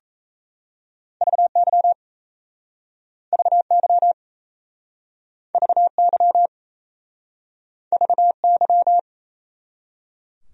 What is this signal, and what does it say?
Morse code, heard as a keyed tone, about 38 wpm, 700 Hz. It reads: VY VY VY VY